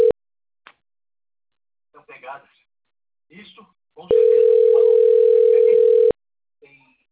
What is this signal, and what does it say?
Ringback tone (North America), 440 + 480 Hz, with speech.
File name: out- - - - - .71802